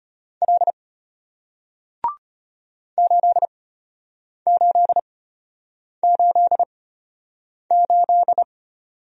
Morse code, heard as a keyed tone, about 35 wpm, spaced as for 8 wpm, 700 Hz. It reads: L 8888